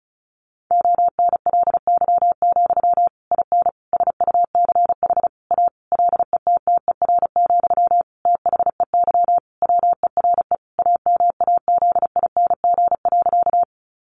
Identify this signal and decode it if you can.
Morse code, heard as a keyed tone, about 34 wpm, 700 Hz. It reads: ONLY, IN SUCH A LETTER, THEY WERE AMAZING.
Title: Sentences from Top 1500 Words 35wpm